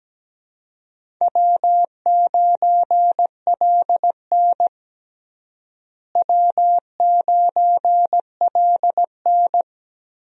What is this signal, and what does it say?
Morse code, heard as a keyed tone, about 17 wpm, 700 Hz. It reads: W9LN W9LN
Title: USA Call Signs - 4 Characters - Repeated 3x - 17wpm